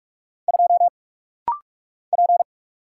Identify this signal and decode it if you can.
Morse code, heard as a keyed tone, about 44 wpm, 690 Hz. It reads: 2 P
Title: Single Letter-Number - Rapid-Fire 45wpm